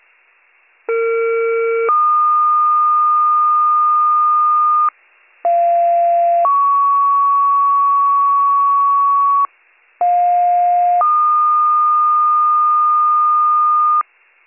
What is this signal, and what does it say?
Two-tone page, heard as a one-second tone, then a three-second tone, about 460 then 1220 Hz, about 690 then 1120 Hz, about 710 then 1220 Hz. It is heard from 0.9 to 14.0 s.